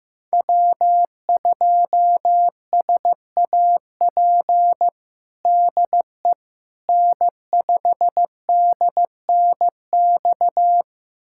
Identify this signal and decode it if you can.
Morse code, heard as a keyed tone, about 15 wpm, 700 Hz. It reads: W2SAP DE N5DNX
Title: USA Callsign DE Callsign - 5 Characters - 15wpm